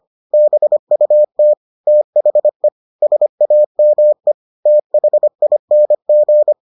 Morse code, heard as a keyed tone, about 25 wpm, 600 Hz. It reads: BUT THE SAME THING